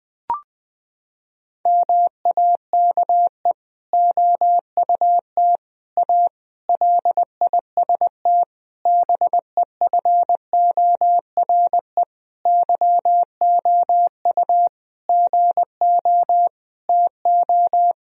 Morse code, heard as a keyed tone, about 20 wpm, 700 Hz. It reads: MAKE OUT A LIST BEFORE YOU GO TO
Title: Sentences from Top 1000 Words - Review Part 2 - 20wpm